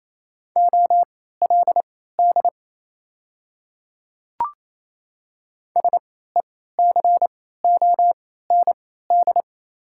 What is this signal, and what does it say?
Morse code, heard as a keyed tone, about 28 wpm, spaced as for 16 wpm, 700 Hz. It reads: OLD SECOND
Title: 3x Character Spacing - Top 300 Words - 28wpm